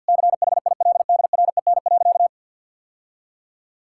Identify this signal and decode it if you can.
Morse code, heard as a keyed tone, about 50 wpm, 700 Hz. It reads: CHILDREN.
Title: Sentences from Top 1000 Words - Review Part 5 - 50wpm